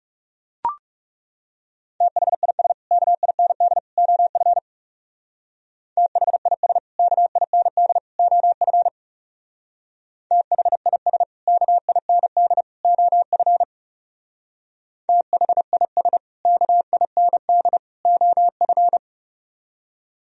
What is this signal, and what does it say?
Morse code, heard as a keyed tone, about 35 wpm, 700 Hz. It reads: THIS KIND OF THIS KIND OF THIS KIND OF THIS KIND OF
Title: Speed-Racing - Sets of 3 Words 30wpm